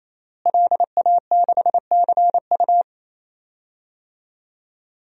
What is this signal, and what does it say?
Morse code, heard as a keyed tone, about 27 wpm, 700 Hz. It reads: LA6CU
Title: DXCC-100 Call Signs - 5 Characters - Repeated 3x - 28wpm